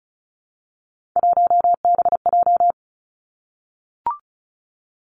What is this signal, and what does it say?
Morse code, heard as a keyed tone, about 36 wpm, 700 Hz. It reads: 1BJ
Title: Three Challenging Letter-Number Combinations 35wpm